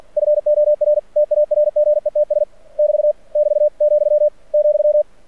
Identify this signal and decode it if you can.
Morse code, heard as a keyed tone, about 46 wpm, 600 Hz. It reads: FOR TARGETS <BT> <BT> , ,
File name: archived / CRSnet